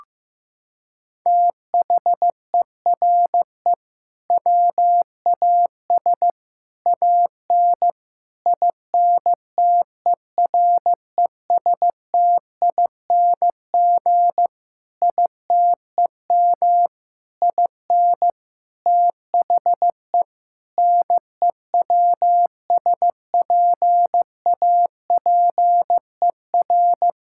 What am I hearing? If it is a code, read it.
Morse code, heard as a keyed tone, about 16 wpm, 700 Hz. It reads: THERE WAS AN INTERESTING ITEM IN THE NEWSPAPER
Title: Sentences from Top 1000 Words - Review Part 6 - 15wpm